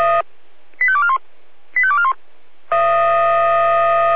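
Five-tone selective call (ZVEI), heard as quick runs of five tones, tones of about 70 ms. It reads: 85241 85241
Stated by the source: ZVEI five-tone selective call 85886 received with an SDR.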